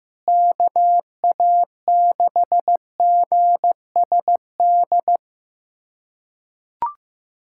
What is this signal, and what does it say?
Morse code, heard as a keyed tone, about 15 wpm, 700 Hz. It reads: KA6GSD
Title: USA Callsign DE Callsign - 6 Characters - 15wpm